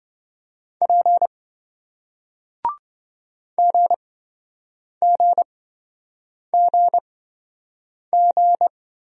Morse code, heard as a keyed tone, about 27 wpm, spaced as for 7 wpm, 700 Hz. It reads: P GGGG